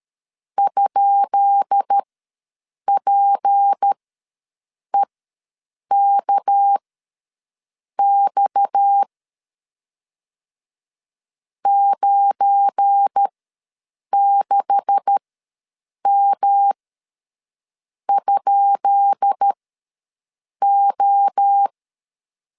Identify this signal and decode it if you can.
Morse code, heard as a keyed tone, about 13 wpm, spaced as for 6 wpm, 800 Hz. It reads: ?PEKX 96M?O
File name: LEZIONI / 10 SIGNS / 10B INT ws 12wpm 3527